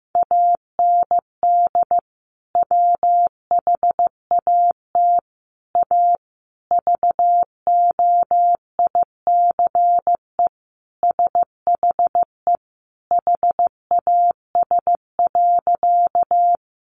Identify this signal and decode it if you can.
Morse code, heard as a keyed tone, about 15 wpm, 700 Hz. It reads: AND WHAT A VOICE SHE HAS.
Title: Sentences from Top 500 Words 15wpm